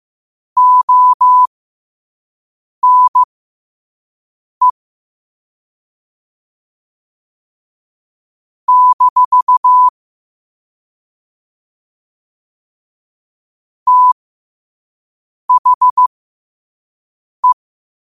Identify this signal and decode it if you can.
Morse code, heard as a keyed tone, about 15 wpm, spaced as for 5 wpm, 1000 Hz. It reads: ONE - THE